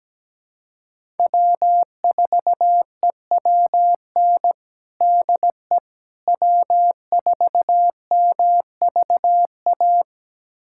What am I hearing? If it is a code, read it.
Morse code, heard as a keyed tone, about 17 wpm, 700 Hz. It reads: W4EWN DE W4MVA
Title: USA Callsign DE Callsign - 5 Characters - 17wpm